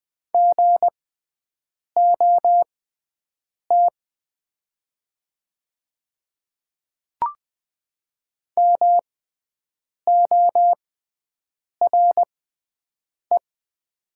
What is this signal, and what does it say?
Morse code, heard as a keyed tone, about 20 wpm, spaced as for 7 wpm, 700 Hz. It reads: GOT MORE
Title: 6x Character Spacing - Top 300 Words - 20wpm